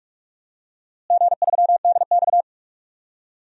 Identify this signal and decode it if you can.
Morse code, heard as a keyed tone, about 46 wpm, 700 Hz. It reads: G3DX
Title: DXCC-100 Call Signs - 4 Characters - Repeated 2x - 45wpm